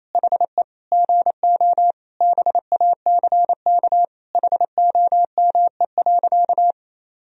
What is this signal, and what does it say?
Morse code, heard as a keyed tone, about 28 wpm, 700 Hz. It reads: HE GO BACK HOME.
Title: Sentences from Top 500 Words and Abbreviations 28wpm